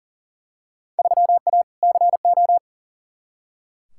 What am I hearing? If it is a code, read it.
Morse code, heard as a keyed tone, about 40 wpm, 700 Hz. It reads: 3A CO
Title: Field Day Exchanges Encore - 40wpm